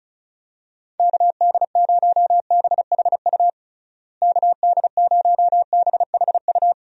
Morse code, heard as a keyed tone, about 34 wpm, 700 Hz. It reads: KD0BHU KD0BHU
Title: USA Call Signs - 6 Characters - Repeated 2x - 35wpm